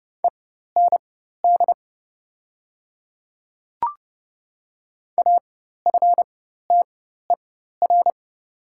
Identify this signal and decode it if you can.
Morse code, heard as a keyed tone, about 30 wpm, spaced as for 14 wpm, 700 Hz. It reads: END AFTER
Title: 4x Character Spacing - Top 500 Words - 30wpm